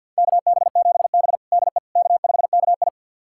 Morse code, heard as a keyed tone, about 52 wpm, 700 Hz. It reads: KB7B DE K5KI
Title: USA Callsign DE Callsign - 4 Characters - 50wpm